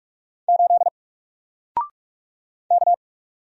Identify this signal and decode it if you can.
Morse code, heard as a keyed tone, about 48 wpm, 700 Hz. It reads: 8 K
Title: Single Letter-Number Alt 45wpm